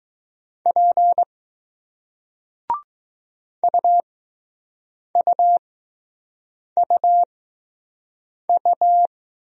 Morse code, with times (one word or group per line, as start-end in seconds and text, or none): 0.7-1.2 P
3.6-9.1 UUUU